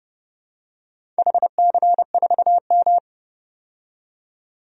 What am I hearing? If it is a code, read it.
Morse code, heard as a keyed tone, about 29 wpm, 700 Hz. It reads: HC4M
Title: DXCC-100 Call Signs - 4 Characters - 30wpm